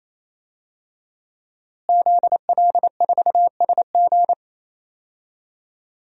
Morse code, heard as a keyed tone, about 28 wpm, 700 Hz. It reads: ZL4SG
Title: DXCC-100 Call Signs - 5 Characters - Repeated 3x - 28wpm